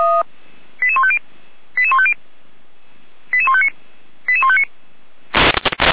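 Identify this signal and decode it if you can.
Five-tone selective call (ZVEI), heard as quick runs of five tones, tones of about 70 ms. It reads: 88150 88150 88159 88159
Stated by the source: ZVEI five-tone call 88419 received by SDR.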